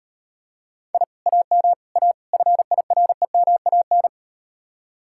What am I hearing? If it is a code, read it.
Morse code, heard as a keyed tone, about 38 wpm, 700 Hz. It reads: I AM A FIREMAN